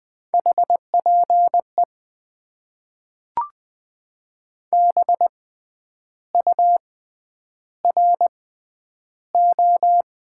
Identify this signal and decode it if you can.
Morse code, heard as a keyed tone, about 21 wpm, spaced as for 9 wpm, 700 Hz. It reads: HPE BURO
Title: Top 90 QSO Elements - 6x Character Spacing - 20wpm